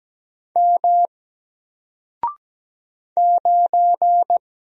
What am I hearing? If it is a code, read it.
Morse code, heard as a keyed tone, about 17 wpm, 700 Hz. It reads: M 9
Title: Single Letter-Number Alt 17wpm